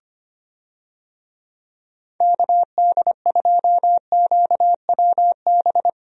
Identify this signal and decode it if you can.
Morse code, heard as a keyed tone, about 25 wpm, 700 Hz. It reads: KD2QWB
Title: USA Call Signs - 6 Characters - Repeated 3x - 25wpm